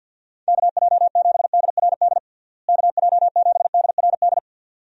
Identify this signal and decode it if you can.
Morse code, heard as a keyed tone, about 53 wpm, 700 Hz. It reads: KJ7DRD KJ7DRD
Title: USA Call Signs - 6 Characters - Repeated 2x - 50wpm